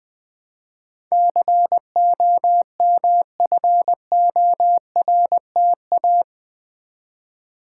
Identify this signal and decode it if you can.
Morse code, heard as a keyed tone, about 19 wpm, 700 Hz. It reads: COMFORTA
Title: Build A Word - 4 Letters At A Time - 11-Letter Words - 20wpm